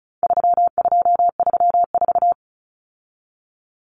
Morse code, heard as a keyed tone, about 34 wpm, 700 Hz. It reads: 3234
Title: Common Cut Numbers - 4-Digit 35wpm